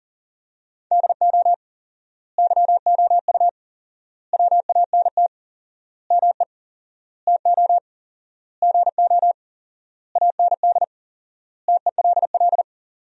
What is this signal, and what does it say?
Morse code, heard as a keyed tone, about 40 wpm, 700 Hz. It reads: DO YOU WANT ME TO GO AND TELL
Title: 4x Word-Spacing - Sentences from Top 200 Words - 40wpm